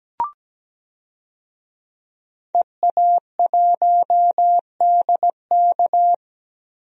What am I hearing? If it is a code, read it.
Morse code, heard as a keyed tone, about 17 wpm, 700 Hz. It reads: EA1DK
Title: DXCC-100 Call Signs - 5 Characters - 17wpm